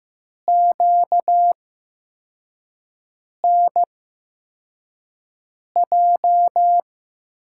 Morse code, heard as a keyed tone, about 15 wpm, spaced as for 4 wpm, 700 Hz. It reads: QNJ
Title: USA Call Signs - 6 Characters - 8x Character Spacing - 15wpm